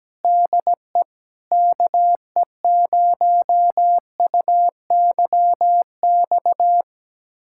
Morse code, heard as a keyed tone, about 17 wpm, 700 Hz. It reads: DE KE0UYX
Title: USA Callsign DE Callsign - 6 Characters - 17wpm